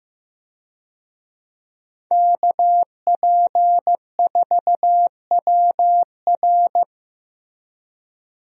Morse code, heard as a keyed tone, about 16 wpm, 700 Hz. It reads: KP4WR
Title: USA Call Signs - 5 Characters - Repeated 2x - 15wpm